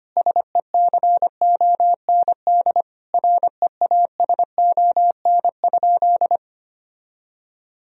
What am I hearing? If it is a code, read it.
Morse code, heard as a keyed tone, about 25 wpm, 700 Hz. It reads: SECOND REASON?
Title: Sentences from Top 1000 Words - Review Part 1 - 25wpm